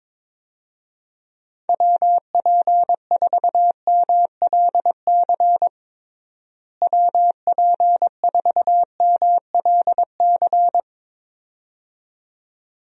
Morse code, heard as a keyed tone, about 22 wpm, 700 Hz. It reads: WP4MLC WP4MLC
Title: USA Call Signs - 6 Characters - Repeated 2x - 22wpm